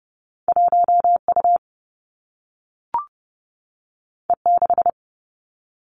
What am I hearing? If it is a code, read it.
Morse code, heard as a keyed tone, about 30 wpm, 700 Hz. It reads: 1U E6
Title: Two Challenging Letter-Number Combinations 30wpm